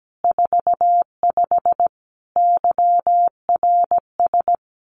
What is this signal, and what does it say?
Morse code, heard as a keyed tone, about 17 wpm, 700 Hz. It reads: 45 YRS
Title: Common QSO Phrases Encore 17wpm